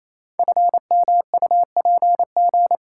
Morse code, heard as a keyed tone, about 28 wpm, 700 Hz. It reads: FMUPG
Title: Code Groups - 5 Letters - 28wpm